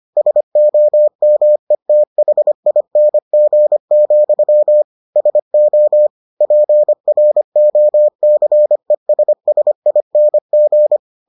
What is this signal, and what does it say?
Morse code, heard as a keyed tone, about 25 wpm, 600 Hz. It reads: SOMETHING, SO PROCESSING